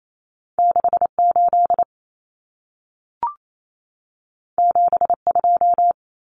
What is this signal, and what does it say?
Morse code, heard as a keyed tone, about 28 wpm, 700 Hz. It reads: 68 72
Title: Common Cut Numbers - 2-Digit 28wpm